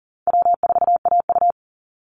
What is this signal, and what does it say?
Morse code, heard as a keyed tone, about 40 wpm, 700 Hz. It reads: W4AU
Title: Call Signs - 40wpm